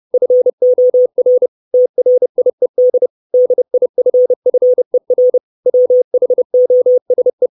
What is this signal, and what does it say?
Morse code, heard as a keyed tone, about 30 wpm, 500 Hz. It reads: FOR TRIED DIFFER WHOSE
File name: 632 Words 30WPM